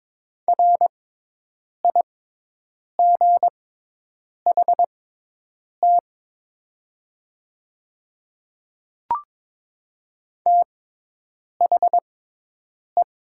Morse code, heard as a keyed tone, about 22 wpm, spaced as for 8 wpm, 700 Hz. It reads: RIGHT THE
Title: 6x Character Spacing - Top 200 Words - 22wpm